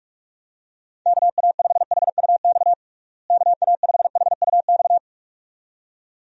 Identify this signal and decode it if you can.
Morse code, heard as a keyed tone, about 41 wpm, 700 Hz. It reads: KA5HUX KA5HUX
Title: USA Call Signs - 6 Characters - Repeated 2x - 45wpm